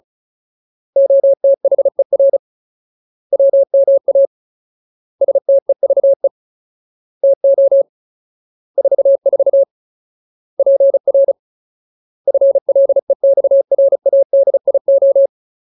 Morse code, heard as a keyed tone, about 34 wpm, 550 Hz. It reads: OTHER WMA STEVE TO 44 PR FLEXRADIO
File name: Ultimate Word List / 35WPM Ultimate